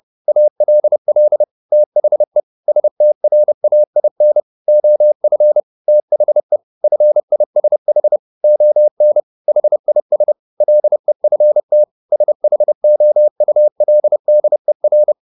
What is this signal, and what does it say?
Morse code, heard as a keyed tone, about 30 wpm, 600 Hz. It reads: ALL THE STRAIN OF THE FISH ON HIS LEFT SHOULDER
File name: the old man and the sea / the old man and the sea 1 30wpm